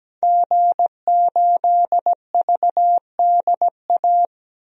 Morse code, heard as a keyed tone, about 17 wpm, 700 Hz. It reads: G8VDA
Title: DXCC-100 Call Signs - 5 Characters - 17wpm